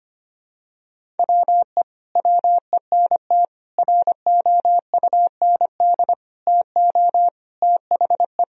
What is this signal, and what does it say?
Morse code, heard as a keyed tone, about 25 wpm, 700 Hz. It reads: WE WENT ROUND TO THE
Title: Sentences from Top 1000 Words - Review Part 1 - 25wpm